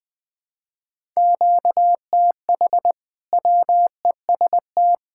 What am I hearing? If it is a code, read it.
Morse code, heard as a keyed tone, about 20 wpm, 700 Hz. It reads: QTH WEST
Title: Practice QSO - Exchange 1 - Beginning - 20wpm